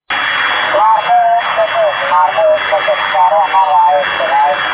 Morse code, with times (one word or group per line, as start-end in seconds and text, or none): none